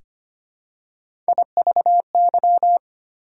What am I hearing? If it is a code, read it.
Morse code, heard as a keyed tone, about 25 wpm, 700 Hz. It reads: IVY
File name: practice / 25wpm / Difficult Words with Hints - 25wpm